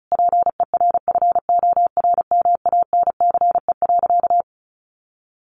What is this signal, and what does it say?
Morse code, heard as a keyed tone, about 34 wpm, 700 Hz. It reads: PERFORMANCE.
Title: Sentences from Top 1000 Words 35wpm